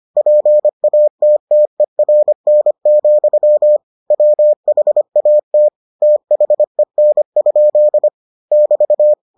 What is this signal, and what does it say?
Morse code, heard as a keyed tone, about 25 wpm, 600 Hz. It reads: PATTERN, WHAT THEN? <BT>